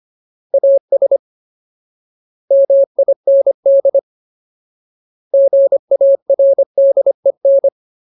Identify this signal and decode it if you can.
Morse code, heard as a keyed tone, about 25 wpm, 550 Hz. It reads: AS MIND GARDEN
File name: Ultimate Word List / 25WPM Ultimate